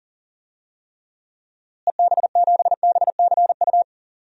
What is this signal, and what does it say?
Morse code, heard as a keyed tone, about 40 wpm, 700 Hz. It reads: EB7BCU
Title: DXCC-100 Call Signs - 6 Characters - Repeated 3x - 40wpm